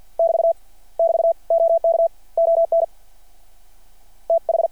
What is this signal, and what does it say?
Morse code, heard as a keyed tone, about 52 wpm, 660 Hz. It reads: <BT> <BT> OK ON TH
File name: archived / CRSnet